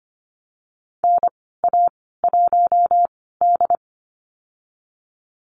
Morse code, heard as a keyed tone, about 25 wpm, spaced as for 16 wpm, 700 Hz. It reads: NA1D
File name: Lesson 15g - Number 5 - Call Signs 25wpm